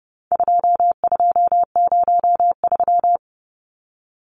Morse code, heard as a keyed tone, about 30 wpm, 700 Hz. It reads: 2203
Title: Numbers - 4-Digit 30wpm